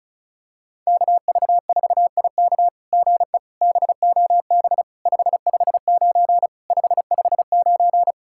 Morse code, heard as a keyed tone, about 36 wpm, 700 Hz. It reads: KV4IK GE BOB 559 559
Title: SOTA QSO Practice 35wpm